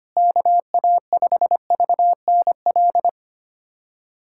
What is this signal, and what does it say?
Morse code, heard as a keyed tone, about 25 wpm, 700 Hz. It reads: KA5VNL
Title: USA Callsign DE Callsign - 6 Characters - 25wpm